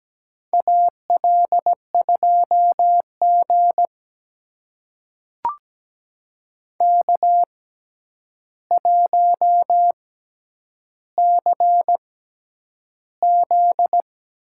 Morse code, heard as a keyed tone, about 17 wpm, spaced as for 9 wpm, 700 Hz. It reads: AL2G K1CZ
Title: USA Call Signs - 4 Characters - 6x Character Spacing - 17wpm